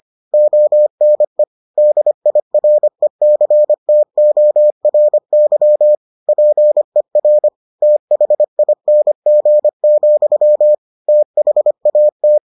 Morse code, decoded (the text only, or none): ONE DIRECTORY PER THING, THAT